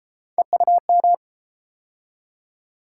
Morse code, heard as a keyed tone, about 31 wpm, 700 Hz. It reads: EUM